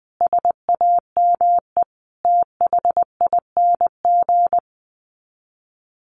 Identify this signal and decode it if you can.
Morse code, heard as a keyed tone, about 20 wpm, 700 Hz. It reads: SAME THING
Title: Sets of 5 Words Encore 20wpm